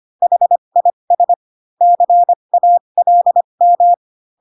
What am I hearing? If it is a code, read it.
Morse code, heard as a keyed tone, about 25 wpm, spaced as for 22 wpm, 700 Hz. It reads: HIS CALM
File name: Pinocchio / Pinocchio ch4 18 25 ik1hge